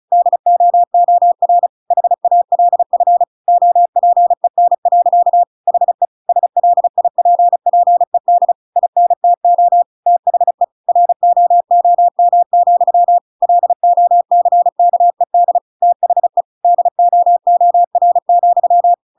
Morse code, heard as a keyed tone, about 35 wpm, 700 Hz. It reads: DOOR HALF OPEN. HE SLIPPED INTO THE ROOM, LOCKED THE DOOR,